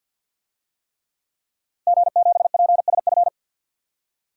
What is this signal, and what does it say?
Morse code, heard as a keyed tone, about 52 wpm, 700 Hz. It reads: G7PSF
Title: DXCC-100 Call Signs - 5 Characters - Repeated 3x - 50wpm